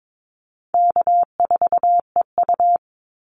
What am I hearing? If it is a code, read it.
Morse code, heard as a keyed tone, about 22 wpm, 700 Hz. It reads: K4EU
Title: Easy Call Signs - 22wpm